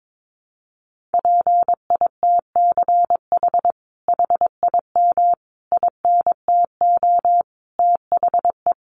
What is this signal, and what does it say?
Morse code, heard as a keyed tone, about 22 wpm, 700 Hz. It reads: PITCH HIM INTO THE
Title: Sentences from Top 1300 Words 22wpm